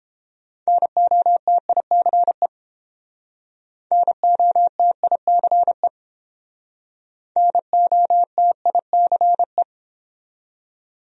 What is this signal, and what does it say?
Morse code, heard as a keyed tone, about 30 wpm, 700 Hz. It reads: NOTICE NOTICE NOTICE